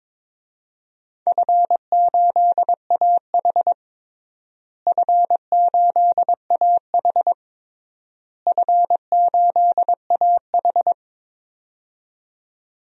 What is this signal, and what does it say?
Morse code, heard as a keyed tone, about 22 wpm, 700 Hz. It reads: F8AH F8AH F8AH
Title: DXCC-100 Call Signs - 4 Characters - Repeated 3x - 22wpm